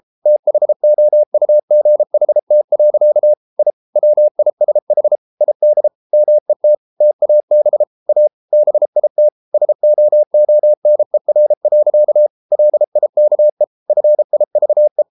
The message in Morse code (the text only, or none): THOUGHT. I WISH ID MET TAB A BIT SOONER. LIKE FIVE